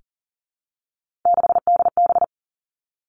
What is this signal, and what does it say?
Morse code, heard as a keyed tone, about 43 wpm, 700 Hz. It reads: 6DB
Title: Three Challenging Letter-Number Combinations 40wpm